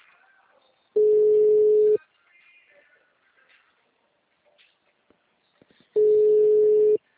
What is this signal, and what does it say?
Ringback tone (Europe, 425 Hz).